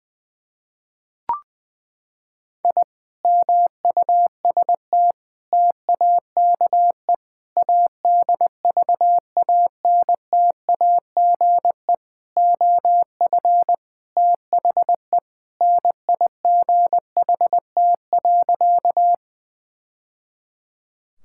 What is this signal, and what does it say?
Morse code, heard as a keyed tone, about 21 wpm, 700 Hz. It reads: I MUST TAKE ADVANTAGE OF THE NIGHT.
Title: Sentences from Top 1000 Words - Review Part 6 - 20wpm